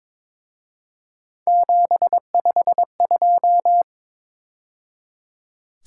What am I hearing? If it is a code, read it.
Morse code, heard as a keyed tone, about 26 wpm, 700 Hz. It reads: 752
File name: Speed-Racing - Three Letter-Number Combinations - 22wpm